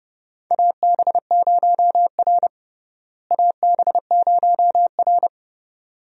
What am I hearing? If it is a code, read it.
Morse code, heard as a keyed tone, about 30 wpm, 700 Hz. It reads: AB0R AB0R
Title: USA Call Signs - 4 Characters - Repeated 2x - 30wpm